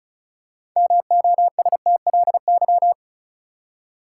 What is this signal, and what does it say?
Morse code, heard as a keyed tone, about 34 wpm, 700 Hz. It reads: MOSTLY
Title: Six Letter Words - 35wpm